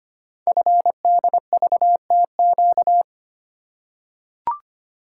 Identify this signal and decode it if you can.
Morse code, heard as a keyed tone, about 25 wpm, 690 Hz. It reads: FDVTQ
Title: Code Groups - 5 Letters - 25wpm